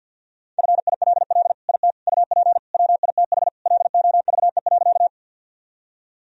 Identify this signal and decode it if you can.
Morse code, heard as a keyed tone, about 47 wpm, 700 Hz. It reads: FILL IT UP WITH LOVE.
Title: Sentences from Top 1000 Words - Review Part 3 - 50wpm